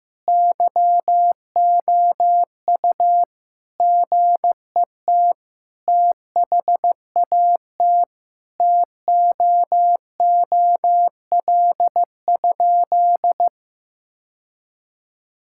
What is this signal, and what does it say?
Morse code, heard as a keyed tone, about 16 wpm, 700 Hz. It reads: YOU GET THAT TOOL?
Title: Sentences from Top 1000 Words - Review Part 6 - 15wpm